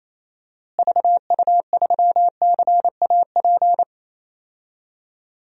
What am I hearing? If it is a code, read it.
Morse code, heard as a keyed tone, about 28 wpm, 700 Hz. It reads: VU3CAP
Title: DXCC-100 Call Signs - 6 Characters - Repeated 3x - 28wpm